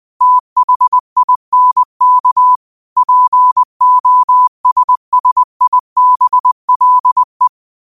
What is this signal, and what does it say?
Morse code, heard as a keyed tone, about 20 wpm, 1000 Hz. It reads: THINK POSSIBLE
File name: qotdfiles / 20wpm